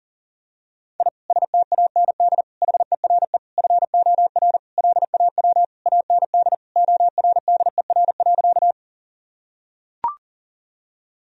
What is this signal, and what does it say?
Morse code, heard as a keyed tone, about 40 wpm, 700 Hz. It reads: I STAND HERE FOR LAW AND ORDER.
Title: Sentences from Top 1000 Words - Review Part 3 - 40wpm